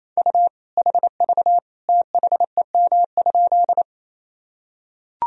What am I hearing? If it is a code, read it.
Morse code, heard as a keyed tone, about 28 wpm, 700 Hz. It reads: U HV THEM?
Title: Sentences from Top 100 Words and Abbreviations 28wpm